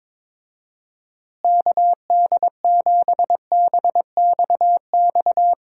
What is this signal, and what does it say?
Morse code, heard as a keyed tone, about 22 wpm, 700 Hz. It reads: KD7BXX
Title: USA Call Signs - 6 Characters - Repeated 3x - 22wpm